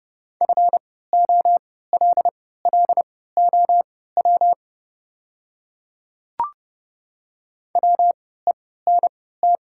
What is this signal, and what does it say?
Morse code, heard as a keyed tone, about 30 wpm, spaced as for 17 wpm, 690 Hz. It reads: FOLLOW WENT
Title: 3x Character Spacing - Top 200 Words - 30wpm